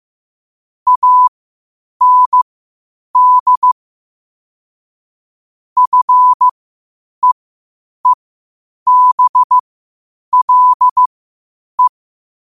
Morse code, heard as a keyed tone, about 15 wpm, spaced as for 8 wpm, 1000 Hz. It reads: AND FEEBLE
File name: qotdfiles / 8wpm